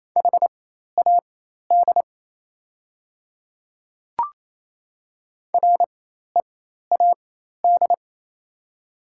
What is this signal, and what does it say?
Morse code, heard as a keyed tone, about 28 wpm, spaced as for 13 wpm, 700 Hz. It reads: HAD READ
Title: 4x Character Spacing - Top 200 Words - 28wpm